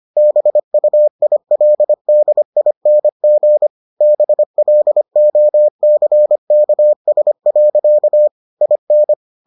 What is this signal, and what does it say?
Morse code, heard as a keyed tone, about 25 wpm, 600 Hz. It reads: BUILDING BLOCKS. IN